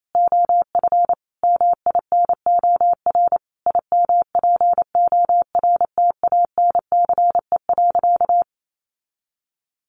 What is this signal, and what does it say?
Morse code, heard as a keyed tone, about 28 wpm, 700 Hz. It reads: OF MINOR IMPORTANCE.
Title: Sentences from Top 1500 Words 28wpm